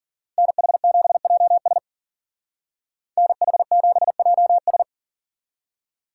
Morse code, heard as a keyed tone, about 49 wpm, 700 Hz. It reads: NH7JS NH7JS
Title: Speed Racing - USA 2x2 Call Signs 40wpm